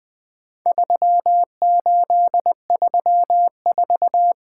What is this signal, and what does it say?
Morse code, heard as a keyed tone, about 20 wpm, 700 Hz. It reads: 3834
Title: Four Letter-Number Combinations - 20wpm